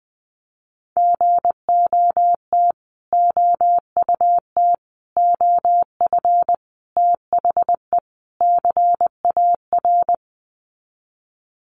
Morse code, heard as a keyed tone, about 20 wpm, 700 Hz. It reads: GOT OUT OF THE CAR
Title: Sets of 5 Words Encore 20wpm